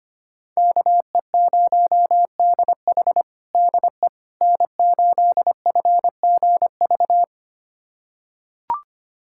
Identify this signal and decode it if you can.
Morse code, heard as a keyed tone, about 25 wpm, 700 Hz. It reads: KE0DH DE N8FGV
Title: USA Callsign DE Callsign - 5 Characters - 25wpm